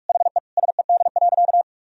Morse code, heard as a keyed tone, about 46 wpm, 700 Hz. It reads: HE SED.
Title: Sentences from Top 200 Words and Abbreviations 45wpm